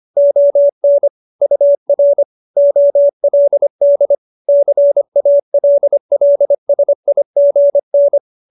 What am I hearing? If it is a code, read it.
Morse code, heard as a keyed tone, about 25 wpm, 570 Hz. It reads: ON UR OLD CALLSIGN
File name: QSO315 25